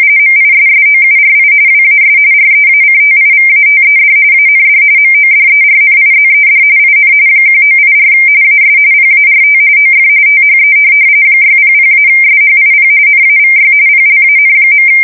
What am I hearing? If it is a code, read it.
RTTY, about 45.45 baud. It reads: MISSOURI WAS REACTIVATED AND MODERNIZED IN 1984 AS PART OF THE 600-SHIP NAVY PLAN.
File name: fsk170 / USS-Missouri